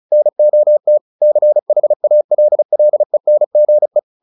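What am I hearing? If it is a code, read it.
Morse code, heard as a keyed tone, about 34 wpm, 600 Hz. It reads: NOT CHALLENGE
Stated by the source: Morse practice at 35 wpm from CW med Gnister, sending Ken Follett's novel Never in English.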